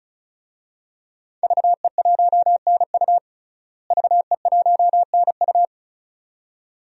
Morse code, heard as a keyed tone, about 36 wpm, 700 Hz. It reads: VE1NU VE1NU
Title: DXCC-100 Call Signs - 5 Characters - Repeated 2x - 35wpm